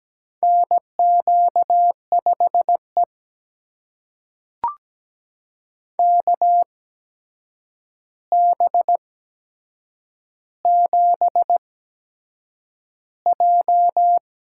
Morse code, heard as a keyed tone, about 17 wpm, spaced as for 7 wpm, 700 Hz. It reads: NQ5E KB7J
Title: USA Call Signs - 4 Characters - 8x Character Spacing - 17wpm